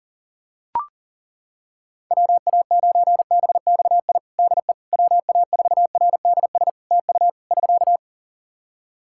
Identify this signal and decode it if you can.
Morse code, heard as a keyed tone, about 40 wpm, 700 Hz. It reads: WA9BXI DE WA4RDS TU <SK>
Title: Practice QSO - Exchange 4 - Ending - 40wpm